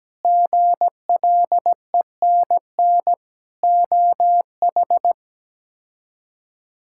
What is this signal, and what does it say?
Morse code, heard as a keyed tone, about 17 wpm, 700 Hz. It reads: GLENN OH
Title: K1USN SST QSO Practice - 17wpm